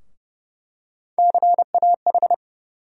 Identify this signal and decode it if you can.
Morse code, heard as a keyed tone, about 30 wpm, 700 Hz. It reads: CAH
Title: USA Call Sign Postfixes - 3 Letters 30wpm